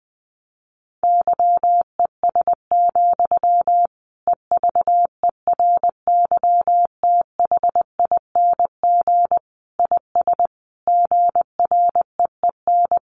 Morse code, heard as a keyed tone, about 20 wpm, 700 Hz. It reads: YES, EVERYTHING IS GREEN
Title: Sentences from Top 700 Words 20wpm